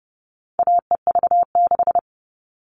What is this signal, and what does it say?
Morse code, heard as a keyed tone, about 30 wpm, 700 Hz. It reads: AEV6
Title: All Cut Numbers - 4-Digit 30wpm